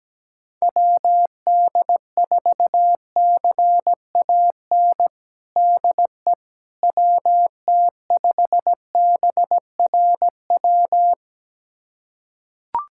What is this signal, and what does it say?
Morse code, heard as a keyed tone, about 17 wpm, 700 Hz. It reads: WD4CAN DE WT5BRW
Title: USA Callsign DE Callsign 17wpm